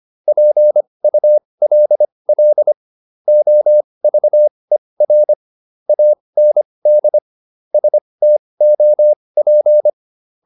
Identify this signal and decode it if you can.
Morse code, heard as a keyed tone, about 25 wpm, spaced as for 20 wpm, 600 Hz. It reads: PULL OVER AND STOP